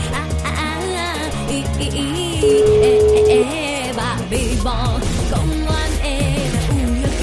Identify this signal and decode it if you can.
Ringback tone (Europe, 425 Hz), with noise.